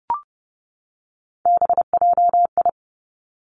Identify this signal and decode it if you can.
Morse code, heard as a keyed tone, about 30 wpm, 700 Hz. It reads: BJI
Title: Three Challenging Letter-Number Combinations 30wpm